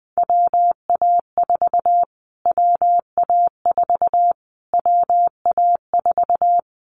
Morse code, heard as a keyed tone, about 20 wpm, 700 Hz. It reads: WA4 WA4 WA4
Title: Callsign Prefixes for Top 25 DXCC - Repeated 3x - 20wpm